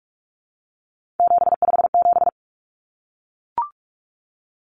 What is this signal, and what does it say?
Morse code, heard as a keyed tone, about 44 wpm, 700 Hz. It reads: 757
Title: Common Cut Numbers - 3-Digit 45wpm